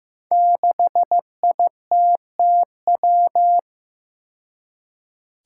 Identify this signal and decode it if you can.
Morse code, heard as a keyed tone, about 15 wpm, 700 Hz. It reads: 6ITTW
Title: Five Challenging Letter-Number Combinations 15wpm